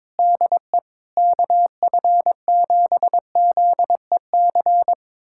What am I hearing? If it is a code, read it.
Morse code, heard as a keyed tone, about 22 wpm, 700 Hz. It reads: DE KF7ZEC
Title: USA Callsign DE Callsign - 6 Characters - 22wpm